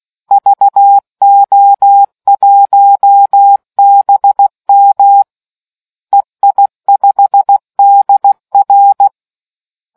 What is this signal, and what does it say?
Morse code, heard as a keyed tone, about 16 wpm, 800 Hz. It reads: VO1BM EI5DR